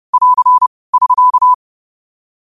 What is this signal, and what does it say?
Morse code, heard as a keyed tone, about 30 wpm, 1000 Hz. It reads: <AR> <SK>